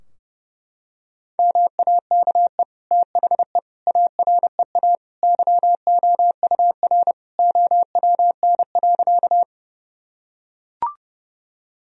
Morse code, heard as a keyed tone, about 32 wpm, 700 Hz. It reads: MAKE THE AREA YOUR OWN.
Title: Sentences from Top 1000 Words - Review Part 2 - 30wpm